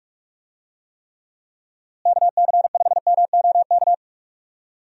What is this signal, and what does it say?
Morse code, heard as a keyed tone, about 46 wpm, 700 Hz. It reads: KC5MOK
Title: USA Call Signs - 6 Characters - Repeated 3x - 45wpm